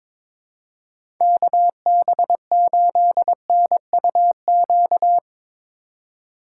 Morse code, heard as a keyed tone, about 22 wpm, 700 Hz. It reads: KB8NUQ
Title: USA Call Signs - 6 Characters - 2x Character Spacing - 22wpm